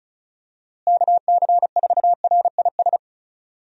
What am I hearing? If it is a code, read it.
Morse code, heard as a keyed tone, about 36 wpm, 700 Hz. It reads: KC4RIS
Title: USA Call Signs - 6 Characters - Repeated 3x - 35wpm